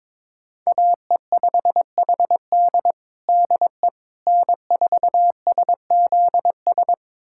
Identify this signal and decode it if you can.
Morse code, heard as a keyed tone, about 21 wpm, 700 Hz. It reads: AE5HD DE N4SZS
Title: USA Callsign DE Callsign - 5 Characters - 22wpm